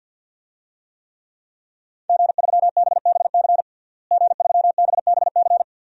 Morse code, heard as a keyed tone, about 52 wpm, 700 Hz. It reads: G3BBC G3BBC
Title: DXCC-100 Call Signs - 5 Characters - Repeated 3x - 50wpm